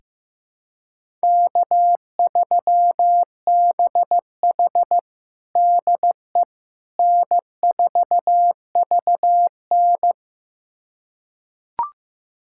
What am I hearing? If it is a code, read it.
Morse code, heard as a keyed tone, about 16 wpm, 700 Hz. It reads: K3BH DE N4VN
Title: USA Callsign DE Callsign - 4 Characters - 15wpm